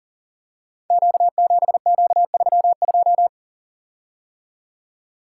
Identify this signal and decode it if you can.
Morse code, heard as a keyed tone, about 40 wpm, 700 Hz. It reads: Q7Q32
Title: Code Groups - 5 Letter:Numbers - 40wpm